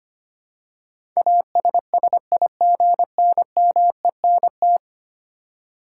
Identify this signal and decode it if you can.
Morse code, heard as a keyed tone, about 25 wpm, 700 Hz. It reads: ASSIGNMENT
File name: Ten Letter Words - 25wpm